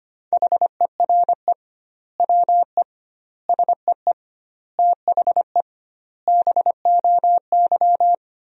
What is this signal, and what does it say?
Morse code, heard as a keyed tone, about 25 wpm, 700 Hz. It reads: HERE WE SEE THE BOY
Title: 2x Word-Spacing - Sentences from Top 300 Words - 25wpm